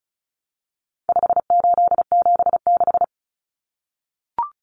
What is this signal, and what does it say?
Morse code, heard as a keyed tone, about 34 wpm, 700 Hz. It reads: 5876
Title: Numbers - 4-Digit 35wpm